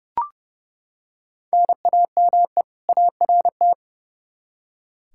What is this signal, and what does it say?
Morse code, heard as a keyed tone, about 32 wpm, 700 Hz. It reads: NAME ART
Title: K1USN SST QSO Practice 30wpm